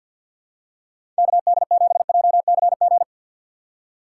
Morse code, heard as a keyed tone, about 51 wpm, 700 Hz. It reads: KD7JCG
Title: USA Call Signs - 6 Characters - 4x Character Spacing - 50wpm